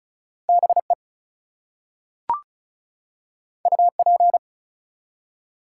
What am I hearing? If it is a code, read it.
Morse code, heard as a keyed tone, about 36 wpm, 700 Hz. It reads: BE UP
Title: Two Letter Words - 35wpm